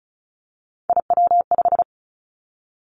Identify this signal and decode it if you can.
Morse code, heard as a keyed tone, about 36 wpm, 700 Hz. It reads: IW5
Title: Three Challenging Letter-Number Combinations 35wpm